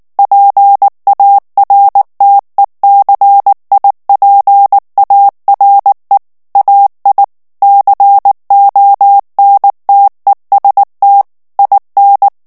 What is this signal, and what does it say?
Morse code, heard as a keyed tone, about 19 wpm, 800 Hz. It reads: PARTECIPARE AI CONTEST IN
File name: morse mp3 / ITALIEN texte clair 2 vitesse 18 mots minute